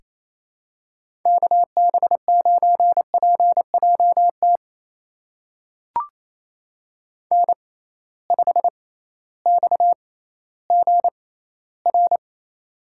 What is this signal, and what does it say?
Morse code, heard as a keyed tone, about 28 wpm, spaced as for 15 wpm, 700 Hz. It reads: KB9PJT N5XGR
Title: USA Call Signs - 6x Character Spacing - 28wpm